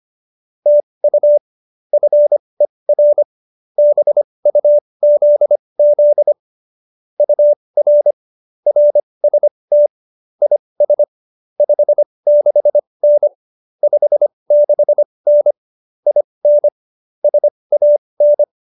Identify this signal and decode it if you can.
Morse code, heard as a keyed tone, about 25 wpm, spaced as for 20 wpm, 590 Hz. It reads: TU FER BUZZ UR RST IS 56N 56N IN SAN
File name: QSO317 20